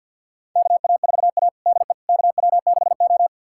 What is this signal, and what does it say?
Morse code, heard as a keyed tone, about 51 wpm, 700 Hz. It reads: KA4A DE KW6O
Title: USA Callsign DE Callsign - 4 Characters - 50wpm